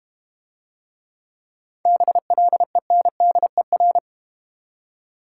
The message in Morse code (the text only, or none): BLENDER